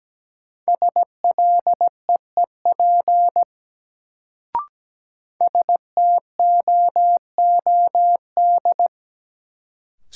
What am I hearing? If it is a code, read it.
Morse code, heard as a keyed tone, about 17 wpm, 700 Hz. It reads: SLEEP STOOD
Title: Sentences from Top 500 Words - New Word Drill - 17wpm